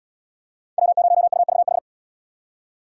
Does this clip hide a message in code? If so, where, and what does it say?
Yes: Morse code from 0.8 to 1.8 s, heard as a keyed tone, about 69 wpm, 700 Hz. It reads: W0UVH